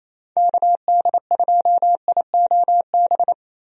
Morse code, heard as a keyed tone, about 28 wpm, 700 Hz. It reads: KD2IOB
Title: USA Call Signs - 6 Characters - Repeated 3x - 28wpm